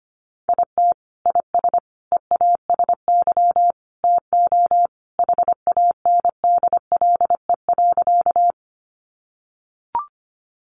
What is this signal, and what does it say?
Morse code, heard as a keyed tone, about 25 wpm, 700 Hz. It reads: IT IS EASY TO HANDLE.
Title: Sentences from Top 1100 Words 25wpm